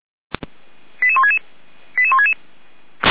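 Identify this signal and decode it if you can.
Five-tone selective call (ZVEI), heard as quick runs of five tones, tones of about 70 ms. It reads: 88166 88166
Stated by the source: ZVEI five-tone selective call received with an SDR.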